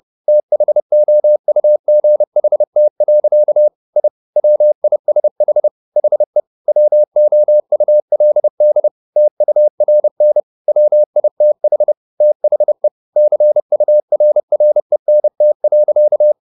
Morse code, heard as a keyed tone, about 30 wpm, 600 Hz. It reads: THOUGHT. I WISH HE WOULD TURN WITH THE CURRENT.
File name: the old man and the sea / the old man and the sea 1 30wpm